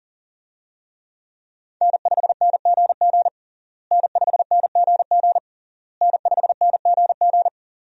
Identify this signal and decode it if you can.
Morse code, heard as a keyed tone, about 40 wpm, 700 Hz. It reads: N5NGG N5NGG N5NGG
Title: USA Call Signs - 5 Characters - Repeated 3x - 40wpm